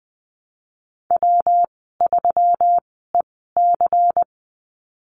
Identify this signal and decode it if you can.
Morse code, heard as a keyed tone, about 20 wpm, spaced as for 15 wpm, 700 Hz. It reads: W3EC
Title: 140 Lesson 23g - Letter f - Call Signs 20wpm